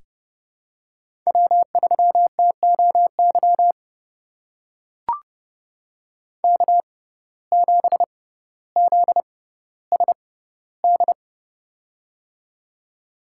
Morse code, heard as a keyed tone, about 30 wpm, spaced as for 15 wpm, 700 Hz. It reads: W3TOY K7ZSD
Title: USA Call Signs - 5 Characters - 6x Character Spacing - 30wpm